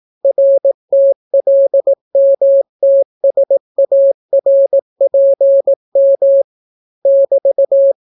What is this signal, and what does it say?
Morse code, heard as a keyed tone, about 18 wpm, 550 Hz. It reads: RTLMTSARPM <BT>